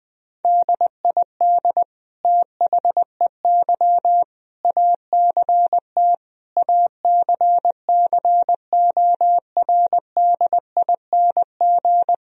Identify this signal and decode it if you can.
Morse code, heard as a keyed tone, about 20 wpm, 700 Hz. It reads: DID THEY ACT ACCORDING
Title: Sentences from Top 1000 Words - Review Part 4 - 20wpm